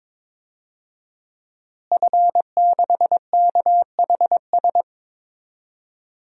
Morse code, heard as a keyed tone, about 22 wpm, 700 Hz. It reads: F6KHS
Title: DXCC-100 Call Signs - 5 Characters - Repeated 2x - 22wpm